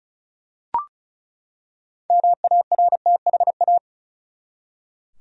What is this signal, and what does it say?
Morse code, heard as a keyed tone, about 36 wpm, 700 Hz. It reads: MARTHA
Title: USA Common Names - 35wpm